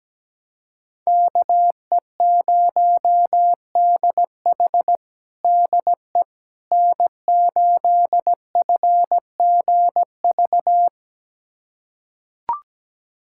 Morse code, heard as a keyed tone, about 17 wpm, 700 Hz. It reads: KE0DH DE N8FGV
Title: USA Callsign DE Callsign - 5 Characters - 17wpm